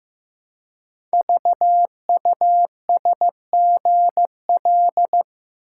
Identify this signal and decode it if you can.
Morse code, heard as a keyed tone, about 15 wpm, 700 Hz. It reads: VUSGL
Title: Code Groups - 5 Letters - 15wpm